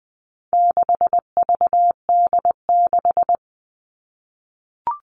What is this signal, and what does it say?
Morse code, heard as a keyed tone, about 20 wpm, 700 Hz. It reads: 6VD6
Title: All Cut Numbers - 4-Digit 20wpm